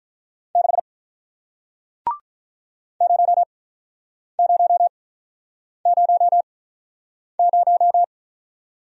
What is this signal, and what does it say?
Morse code, heard as a keyed tone, about 48 wpm, spaced as for 9 wpm, 700 Hz. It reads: 6 0000